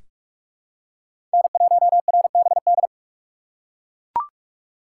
Morse code, heard as a keyed tone, about 42 wpm, 700 Hz. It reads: N1RBD
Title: K1USN SST QSO Practice - 45wpm